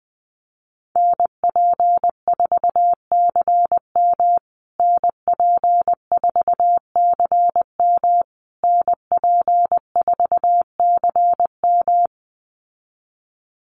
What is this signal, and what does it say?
Morse code, heard as a keyed tone, about 19 wpm, 700 Hz. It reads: NP4CM NP4CM NP4CM
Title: USA Call Signs Repeated 3x 20wpm